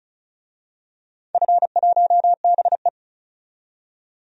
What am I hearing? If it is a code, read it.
Morse code, heard as a keyed tone, about 36 wpm, 700 Hz. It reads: F1BE
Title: DXCC-100 Call Signs - 4 Characters - 35wpm